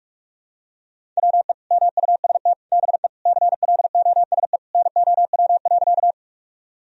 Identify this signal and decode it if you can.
Morse code, heard as a keyed tone, about 46 wpm, 700 Hz. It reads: WE MUST BE CLOSE NOW.
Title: Sentences from Top 1000 Words - Review Part 1 - 45wpm